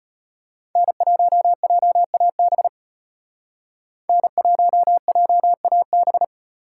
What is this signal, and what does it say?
Morse code, heard as a keyed tone, about 32 wpm, 700 Hz. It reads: N1JAB N1JAB